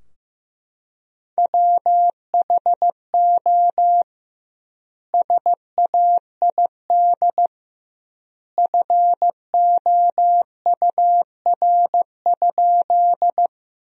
Morse code, heard as a keyed tone, about 15 wpm, 700 Hz. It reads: WHO SAID FOUR?
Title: 2x Word-Spacing - Sentences from Top 300 Words - 15wpm